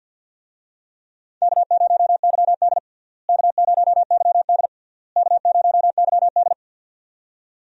Morse code, heard as a keyed tone, about 54 wpm, 700 Hz. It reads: K0YD K0YD K0YD
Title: USA Call Signs - 4 Characters - Repeated 3x - 50wpm